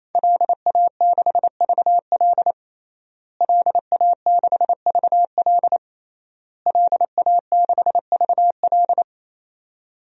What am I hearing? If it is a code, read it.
Morse code, heard as a keyed tone, about 28 wpm, 700 Hz. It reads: LA6VL LA6VL LA6VL
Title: DXCC-100 Call Signs - 5 Characters - Repeated 3x - 28wpm